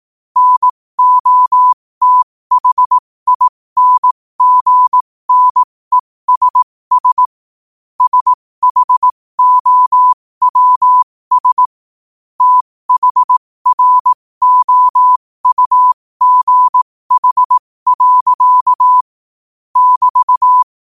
Morse code, heard as a keyed tone, about 18 wpm, spaced as for 15 wpm, 1000 Hz. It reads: NOTHINGNESS SHOWS THROUGH. <BT>
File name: qotdfiles / 15wpm